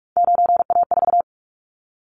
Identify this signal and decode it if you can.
Morse code, heard as a keyed tone, about 46 wpm, 700 Hz. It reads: 9A4
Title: Callsign Prefixes for Top 25 DXCC - 45wpm